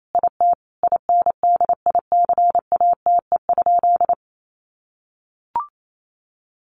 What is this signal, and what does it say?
Morse code, heard as a keyed tone, about 28 wpm, 700 Hz. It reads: IT INDICATE?
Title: Sentences from Top 1300 Words 28wpm